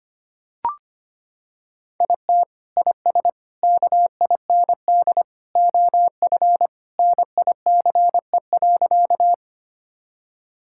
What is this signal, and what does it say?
Morse code, heard as a keyed tone, about 25 wpm, 700 Hz. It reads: IT IS KIND OF NICE.
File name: practice / 25wpm / Sentences with 2-Word Phrases - 25wpm